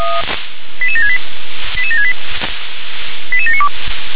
Five-tone selective call (ZVEI), heard as quick runs of five tones, tones of about 80 ms. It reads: 88768 88768 88792
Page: SDR/ZVEI